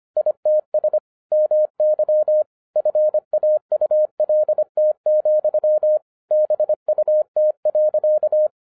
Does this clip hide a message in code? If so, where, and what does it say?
Yes: Morse code from 0.2 to 8.5 s, heard as a keyed tone, about 25 wpm, 610 Hz. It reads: ITS MY FAULT, BUT.